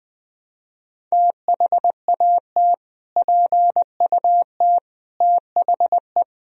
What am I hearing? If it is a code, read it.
Morse code, heard as a keyed tone, about 20 wpm, 700 Hz. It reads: THAT PUT THE
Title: Sentences from Top 1000 Words - Review Part 1 - 20wpm